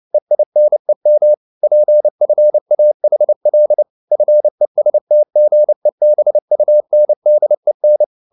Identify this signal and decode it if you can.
Morse code, heard as a keyed tone, about 29 wpm, 600 Hz. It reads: EINEM PFAHL FESTGEBUNDEN